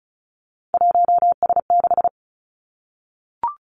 Morse code, heard as a keyed tone, about 36 wpm, 700 Hz. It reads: 1S6
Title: Three Challenging Letter-Number Combinations 35wpm